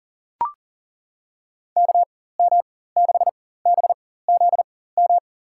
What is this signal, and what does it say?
Morse code, heard as a keyed tone, about 40 wpm, spaced as for 19 wpm, 700 Hz. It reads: KM6BZM
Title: USA Call Signs - 4x Character Spacing - 40wpm